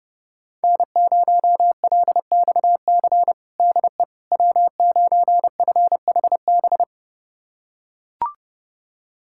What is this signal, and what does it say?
Morse code, heard as a keyed tone, about 32 wpm, 700 Hz. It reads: N0LXC DE W9FHB
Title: USA Callsign DE Callsign - 5 Characters - 30wpm